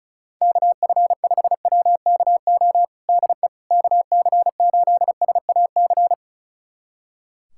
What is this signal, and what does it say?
Morse code, heard as a keyed tone, about 36 wpm, 700 Hz. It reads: KF5WKO DE KC8SAC
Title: USA Callsign DE Callsign 35wpm